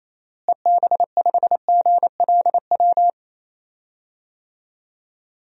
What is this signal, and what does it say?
Morse code, heard as a keyed tone, about 28 wpm, 700 Hz. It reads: EB5GLW
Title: DXCC-100 Call Signs - 6 Characters - Repeated 3x - 28wpm